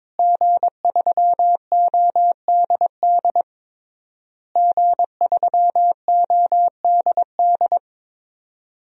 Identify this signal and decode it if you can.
Morse code, heard as a keyed tone, about 22 wpm, 700 Hz. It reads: G3ODD G3ODD
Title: DXCC-100 Call Signs - 5 Characters - Repeated 3x - 22wpm